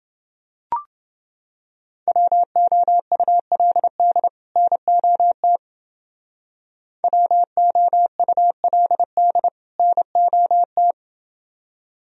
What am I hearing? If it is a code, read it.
Morse code, heard as a keyed tone, about 29 wpm, 700 Hz. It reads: WOULD NOT WOULD NOT